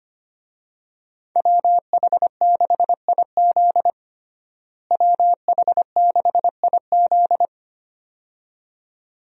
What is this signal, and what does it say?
Morse code, heard as a keyed tone, about 25 wpm, 700 Hz. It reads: WH6IZ WH6IZ
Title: USA Call Signs - 5 Characters - Repeated 2x - 25wpm